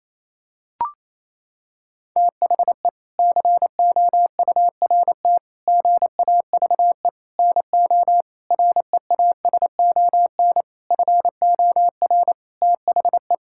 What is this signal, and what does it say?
Morse code, heard as a keyed tone, about 27 wpm, 700 Hz. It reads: THE COURT GAVE NO REASON FOR THE
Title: Sentences from Top 1600 Words 28wpm